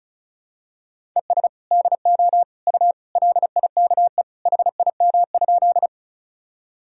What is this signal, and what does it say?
Morse code, heard as a keyed tone, about 34 wpm, 700 Hz. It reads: ES DO U LIKE HIM?
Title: Sentences from Top 100 Words and Abbreviations 35wpm